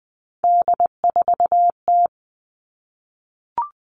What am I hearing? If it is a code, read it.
Morse code, heard as a keyed tone, about 20 wpm, 700 Hz. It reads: D4T
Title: All Cut Numbers - 3-Digit 20wpm